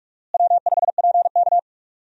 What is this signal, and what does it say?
Morse code, heard as a keyed tone, about 46 wpm, 700 Hz. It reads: W5PK
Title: USA Call Signs - 4 Characters - Repeated 3x - 45wpm